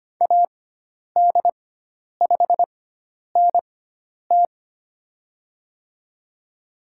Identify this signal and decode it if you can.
Morse code, heard as a keyed tone, about 25 wpm, spaced as for 10 wpm, 700 Hz. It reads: AD5NT
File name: Lesson 16g - Letter c - Call Signs 25wpm